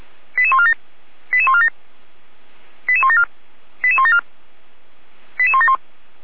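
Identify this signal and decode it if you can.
Five-tone selective call (ZVEI), heard as quick runs of five tones, tones of about 70 ms. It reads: 88147 88147 88174 88174 88171
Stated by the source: ZVEI five-tone call 88012 received by SDR, recorded by feuerwehr-orlamuende fire brigade.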